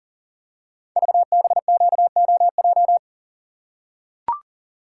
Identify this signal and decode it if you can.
Morse code, heard as a keyed tone, about 40 wpm, 690 Hz. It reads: VBQOJ
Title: Code Groups - 5 Letters - 40wpm